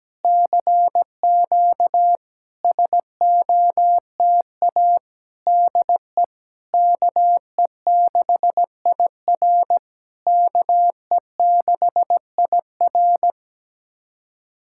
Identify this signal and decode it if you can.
Morse code, heard as a keyed tone, about 17 wpm, 700 Hz. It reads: CQ SOTA DE KE6IR KE6IR
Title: SOTA QSO Practice 17wpm